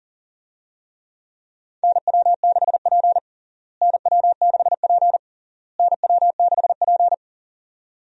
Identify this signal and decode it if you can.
Morse code, heard as a keyed tone, about 37 wpm, 700 Hz. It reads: NW6P NW6P NW6P
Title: USA Call Signs - 4 Characters - Repeated 3x - 40wpm